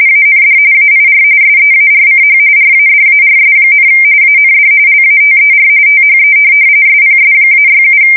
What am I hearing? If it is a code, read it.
RTTY, about 45.45 baud. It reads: AND AIR SEARCH RADARS, AND LATER THE SPS-67
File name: fsk170 / USS-New-Jersey